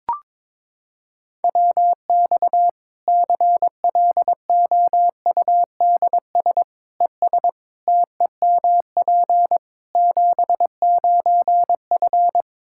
Morse code, heard as a keyed tone, about 22 wpm, 700 Hz. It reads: WX CLOUDS ES TEMP 79F
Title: Practice QSO - Exchange 2 - Info - 22wpm